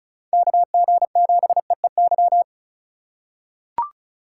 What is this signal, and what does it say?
Morse code, heard as a keyed tone, about 36 wpm, 700 Hz. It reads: KG7EEY
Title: USA Call Signs - 6 Characters - 4x Character Spacing - 35wpm